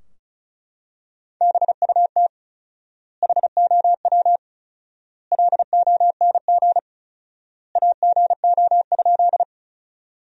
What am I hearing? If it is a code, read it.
Morse code, heard as a keyed tone, about 34 wpm, 700 Hz. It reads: BUT HOW LONG AGO?
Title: 4x Word-Spacing - Sentences from Top 500 Words - 35wpm